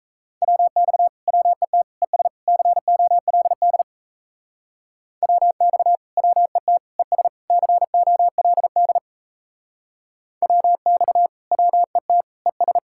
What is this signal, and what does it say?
Morse code, heard as a keyed tone, about 35 wpm, 700 Hz. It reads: WX WET ES COLD WX WET ES COLD WX WET ES
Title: Speed-Racing Common QSO Phrases - 28wpm